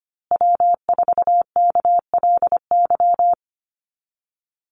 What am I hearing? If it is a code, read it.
Morse code, heard as a keyed tone, about 25 wpm, 700 Hz. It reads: W4KLY
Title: Call Signs Repeated 2x - 25wpm